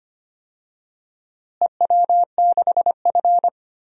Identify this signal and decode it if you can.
Morse code, heard as a keyed tone, about 25 wpm, 700 Hz. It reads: EW6F
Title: DXCC-100 Call Signs - 4 Characters - Repeated 2x - 25wpm